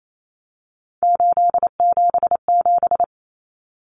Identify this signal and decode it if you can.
Morse code, heard as a keyed tone, about 28 wpm, 700 Hz. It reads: 877
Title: Common Cut Numbers - 3-Digit 28wpm